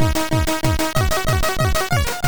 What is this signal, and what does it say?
There is no code.